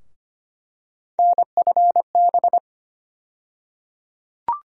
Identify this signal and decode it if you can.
Morse code, heard as a keyed tone, about 25 wpm, 700 Hz. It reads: NFB
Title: USA Call Sign Postfixes - 3 Letters 25wpm